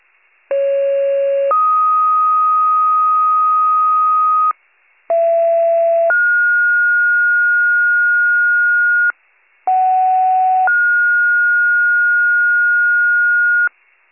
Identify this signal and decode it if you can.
Two-tone page, heard as a one-second tone, then a three-second tone, about 550 then 1250 Hz, about 670 then 1430 Hz, about 750 then 1430 Hz, with noise. It is heard from 0.5 to 13.7 s.